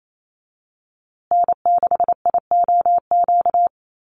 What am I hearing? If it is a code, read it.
Morse code, heard as a keyed tone, about 28 wpm, 700 Hz. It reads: N6IOQ
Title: USA Call Signs Repeated 3x 28wpm